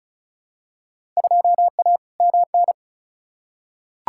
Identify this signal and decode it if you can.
Morse code, heard as a keyed tone, about 34 wpm, 700 Hz. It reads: 2A MN
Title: Field Day Exchanges - 35wpm